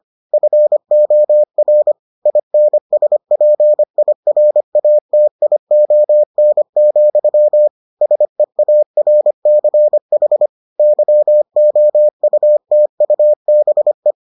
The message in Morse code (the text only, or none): FOR INSPIRATION, SEARCH YOUTUBE